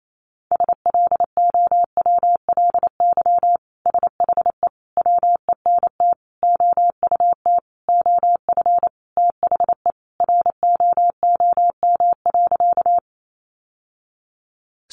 Morse code, heard as a keyed tone, about 28 wpm, 700 Hz. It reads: SLOWLY SHE WENT OUT OF THE ROOM.
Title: Sentences from Top 1100 Words 28wpm